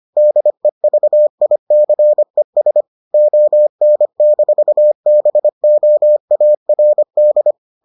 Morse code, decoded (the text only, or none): DEVICES ON-BOARD